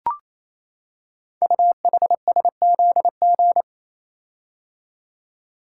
Morse code, heard as a keyed tone, about 29 wpm, 700 Hz. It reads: UHSZG
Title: Code Groups - 5 Letters - 28wpm